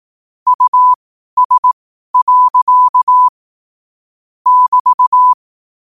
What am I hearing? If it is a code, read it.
Morse code, heard as a keyed tone, about 18 wpm, spaced as for 12 wpm, 1000 Hz. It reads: US. <BT>
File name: qotdfiles / 12wpm